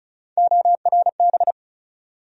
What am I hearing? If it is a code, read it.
Morse code, heard as a keyed tone, about 36 wpm, 700 Hz. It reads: ORB
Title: USA Call Sign Postfixes - 3 Letters 35wpm